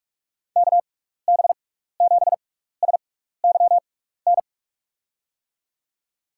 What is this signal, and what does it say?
Morse code, heard as a keyed tone, about 44 wpm, spaced as for 15 wpm, 700 Hz. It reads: KB7SYN
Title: USA Call Signs - 6 Characters - 6x Character Spacing - 45wpm